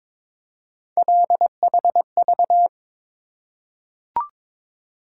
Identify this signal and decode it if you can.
Morse code, heard as a keyed tone, about 22 wpm, 700 Hz. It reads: LHV
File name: practice / 22wpm / Three Letter-Number Combinations - 22wpm